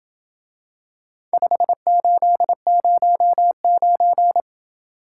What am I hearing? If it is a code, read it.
Morse code, heard as a keyed tone, about 26 wpm, 700 Hz. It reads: 5809